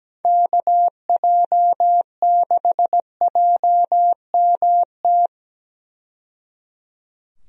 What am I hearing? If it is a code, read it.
Morse code, heard as a keyed tone, about 17 wpm, 700 Hz. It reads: KJ6JMT
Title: USA Call Signs - 6 Characters - 17wpm